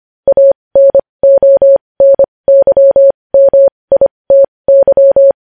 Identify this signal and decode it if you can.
Morse code, heard as a keyed tone, about 25 wpm, spaced as for 20 wpm, 550 Hz. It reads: ANONYMITY
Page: news headlines in morse code at 20 wpm